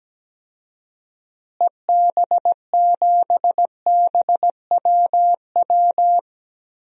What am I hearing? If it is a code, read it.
Morse code, heard as a keyed tone, about 17 wpm, 700 Hz. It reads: EB7BWW
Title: DXCC-100 Call Signs - 6 Characters - Repeated 3x - 17wpm